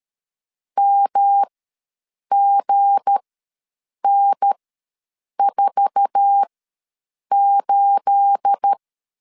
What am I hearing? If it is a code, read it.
Morse code, heard as a keyed tone, about 13 wpm, spaced as for 7 wpm, 800 Hz. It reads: MGN48
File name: LEZIONI / 9 28 ws 12wpm 3354